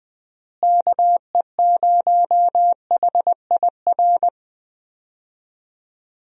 Morse code, heard as a keyed tone, about 20 wpm, 700 Hz. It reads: KE0HIR
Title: USA Call Signs - 6 Characters - Repeated 3x - 20wpm